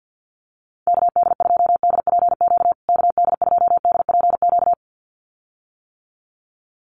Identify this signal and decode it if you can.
Morse code, heard as a keyed tone, about 50 wpm, 700 Hz. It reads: KD2DPQ KD2DPQ
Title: USA Call Signs Repeated 2x 50wpm